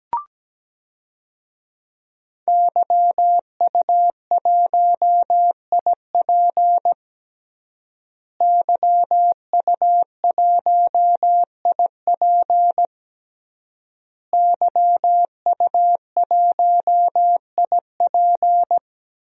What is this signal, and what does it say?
Morse code, heard as a keyed tone, about 17 wpm, 700 Hz. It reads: YU1IP YU1IP YU1IP
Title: DXCC-100 Call Signs - 5 Characters - Repeated 3x - 17wpm